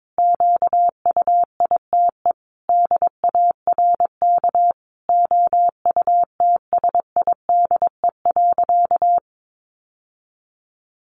Morse code, heard as a keyed tone, about 22 wpm, 700 Hz. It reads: QUITE DARK OUTSIDE.
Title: Sentences from Top 900 Words 22wpm